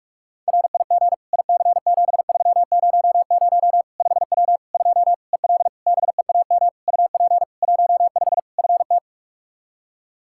Morse code, heard as a keyed tone, about 41 wpm, 700 Hz. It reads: RIG IC7300 5W 2 EL BEAM UP 15 FT
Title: K1USN SST QSO Practice 45wpm